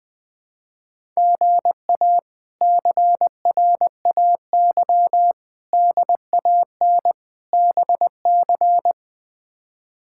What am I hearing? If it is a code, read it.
Morse code, heard as a keyed tone, about 20 wpm, 700 Hz. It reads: GA CRAY DAN BC
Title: K1USN SST QSO Practice - 20wpm